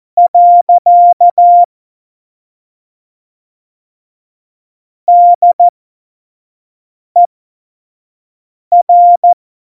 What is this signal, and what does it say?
Morse code, heard as a keyed tone, about 14 wpm, spaced as for 5 wpm, 700 Hz. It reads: . DER
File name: Morsekurs / Witze / Witz 28 Juli 2020, 25 F70